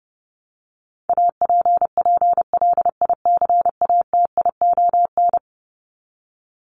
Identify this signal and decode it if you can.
Morse code, heard as a keyed tone, about 30 wpm, 700 Hz. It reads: APPLICATION
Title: Top 1001-1250 Words 30wpm